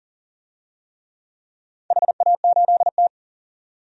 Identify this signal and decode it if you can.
Morse code, heard as a keyed tone, about 42 wpm, 700 Hz. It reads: HA8T
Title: DXCC-100 Call Signs - 4 Characters - 40wpm